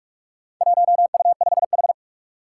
Morse code, heard as a keyed tone, about 46 wpm, 700 Hz. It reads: 1U5H
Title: Four Challenging Letter-Number Combinations 45wpm